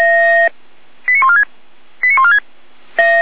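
Five-tone selective call (ZVEI), heard as quick runs of five tones, tones of about 70 ms. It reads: 89146 89146
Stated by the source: ZVEI five-tone selective call received by SDR, logged on the Feuerwehr Orlamünde fire brigade server.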